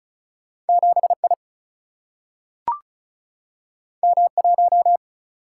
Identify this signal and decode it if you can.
Morse code, heard as a keyed tone, about 36 wpm, 700 Hz. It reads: 7 I M 1
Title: Single-Letter Single-Number 35wpm